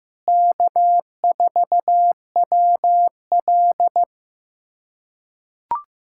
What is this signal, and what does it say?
Morse code, heard as a keyed tone, about 15 wpm, 700 Hz. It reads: K4WL
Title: USA Callsign DE Callsign - 4 Characters - 15wpm